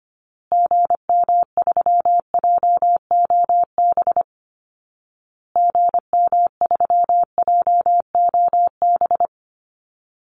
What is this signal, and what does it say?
Morse code, heard as a keyed tone, about 25 wpm, 700 Hz. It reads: GM3JOB GM3JOB
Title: Call Signs Repeated 2x - 25wpm